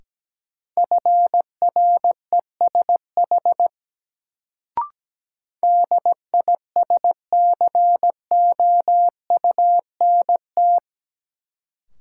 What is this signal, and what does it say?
Morse code, heard as a keyed tone, about 17 wpm, 700 Hz. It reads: FRESH DISCOUNT
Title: Sentences from Top 1200 Words - New Word Drill - 17wpm